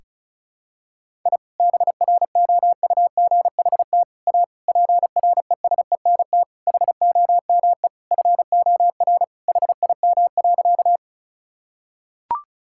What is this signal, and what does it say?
Morse code, heard as a keyed tone, about 36 wpm, 700 Hz. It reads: I BROUGHT A PRESENT HOME FOR HIM.
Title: Sentences from Top 1000 Words - Review Part 5 - 35wpm